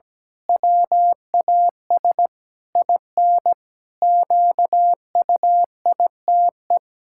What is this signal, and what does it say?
Morse code, heard as a keyed tone, about 17 wpm, 700 Hz. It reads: WAS IN QUITE
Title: Pangram Challenge v2 - 17wpm